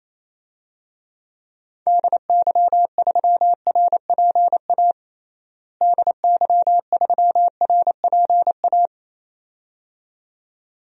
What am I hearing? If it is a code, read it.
Morse code, heard as a keyed tone, about 28 wpm, 700 Hz. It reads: DY3RPA DY3RPA
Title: DXCC-100 Call Signs - 6 Characters - Repeated 2x - 28wpm